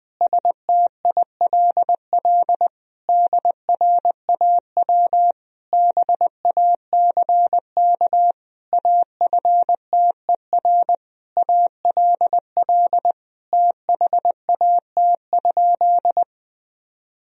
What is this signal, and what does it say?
Morse code, heard as a keyed tone, about 21 wpm, 700 Hz. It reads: STILL DRAW BACK AFTER ALL THAT?
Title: Sentences from Top 1000 Words - Review Part 1 - 20wpm